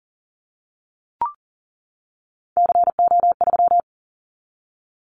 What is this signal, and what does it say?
Morse code, heard as a keyed tone, about 40 wpm, 690 Hz. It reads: CO3
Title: Callsign Prefixes for Top 100 DXCC - 40wpm